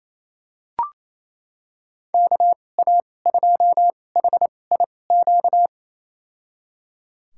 Morse code, heard as a keyed tone, about 28 wpm, spaced as for 20 wpm, 700 Hz. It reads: KA2HIQ
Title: USA Call Signs - 2x Character Spacing - 28wpm